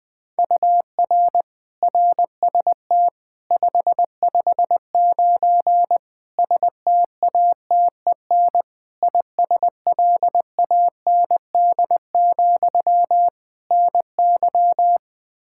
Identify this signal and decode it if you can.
Morse code, heard as a keyed tone, about 20 wpm, 700 Hz. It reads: UR RST 559 STATEN ISLAND, NY
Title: Practice QSO - Exchange 1 - Beginning - 20wpm